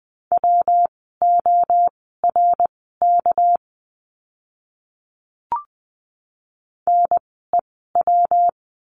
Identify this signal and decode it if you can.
Morse code, heard as a keyed tone, about 20 wpm, spaced as for 14 wpm, 700 Hz. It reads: WORK NEW
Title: 244 Lesson 39f - Letter z - Words 20wpm